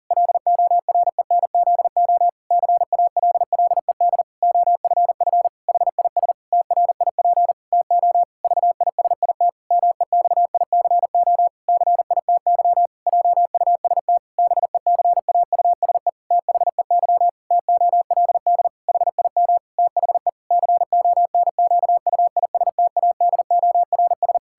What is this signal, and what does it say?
Morse code, heard as a keyed tone, about 40 wpm, 700 Hz. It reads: LORENZO CALLED OFF HIS TRIP TO VISIT MEXICO CITY JUST BECAUSE THEY TOLD HIM THE CONQUISTADORS
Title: Pangram Challenge v2 - 40wpm